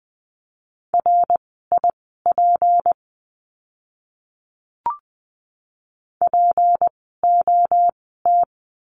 Morse code, heard as a keyed tone, about 20 wpm, spaced as for 14 wpm, 700 Hz. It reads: RIP POT
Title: 153 Lesson 25f - Letter p - Words 20wpm